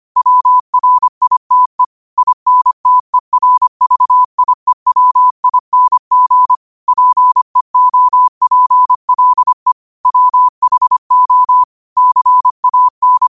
Morse code, heard as a keyed tone, about 25 wpm, 1000 Hz. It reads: WRITE INTERVIEWING PEOPLE WHO CAN